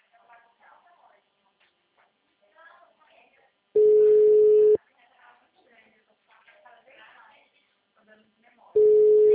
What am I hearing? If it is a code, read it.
Ringback tone (Europe, 425 Hz), with speech.
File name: out- -2002- - - .197